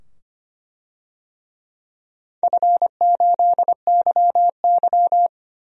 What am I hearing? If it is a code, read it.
Morse code, heard as a keyed tone, about 25 wpm, 700 Hz. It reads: F8YY
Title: DXCC-100 Call Signs - 4 Characters - 25wpm